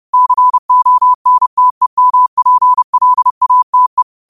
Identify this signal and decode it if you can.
Morse code, heard as a keyed tone, about 30 wpm, 1000 Hz. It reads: CONTEMPLATE
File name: qotdfiles / 30wpm